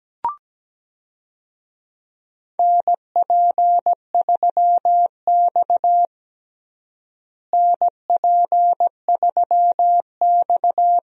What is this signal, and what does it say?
Morse code, heard as a keyed tone, about 17 wpm, 700 Hz. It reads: NP3X NP3X
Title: USA Call Signs - 4 Characters - Repeated 2x - 17wpm